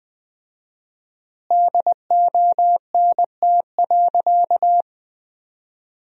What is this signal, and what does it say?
Morse code, heard as a keyed tone, about 21 wpm, 700 Hz. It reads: DONT.
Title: 4x Word-Spacing - Sentences from Top 300 Words - 20wpm